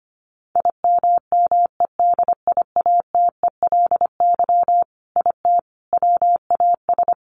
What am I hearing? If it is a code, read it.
Morse code, heard as a keyed tone, about 25 wpm, 700 Hz. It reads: IMMEDIATELY IT WAS
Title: Sentences from Top 1100 Words 25wpm